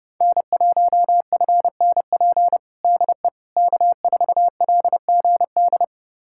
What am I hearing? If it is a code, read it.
Morse code, heard as a keyed tone, about 30 wpm, 700 Hz. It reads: N1FNP DE K4LGD
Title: USA Callsign DE Callsign - 5 Characters - 30wpm